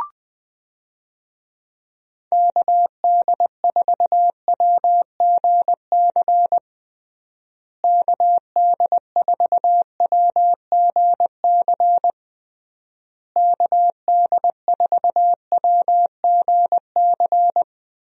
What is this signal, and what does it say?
Morse code, heard as a keyed tone, about 20 wpm, 700 Hz. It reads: KD4WGC KD4WGC KD4WGC
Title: USA Call Signs - 6 Characters - Repeated 3x - 20wpm